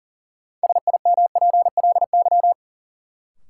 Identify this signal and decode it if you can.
Morse code, heard as a keyed tone, about 40 wpm, 700 Hz. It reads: SIMPLY
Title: Sentences from Top 600 Words - New Word Drill - 40wpm